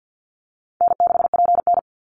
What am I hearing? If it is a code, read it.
Morse code, heard as a keyed tone, about 50 wpm, 700 Hz. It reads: N6PN
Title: Easy Call Signs - 50wpm